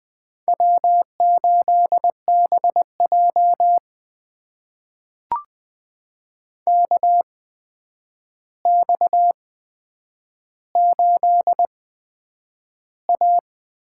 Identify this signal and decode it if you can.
Morse code, heard as a keyed tone, about 20 wpm, spaced as for 9 wpm, 700 Hz. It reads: W8BJ KX8A
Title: USA Call Signs - 4 Characters - 8x Character Spacing - 20wpm